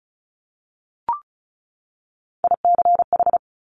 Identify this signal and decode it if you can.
Morse code, heard as a keyed tone, about 36 wpm, 700 Hz. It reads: ICH
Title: Top Three Letter Combinations in Words 35wpm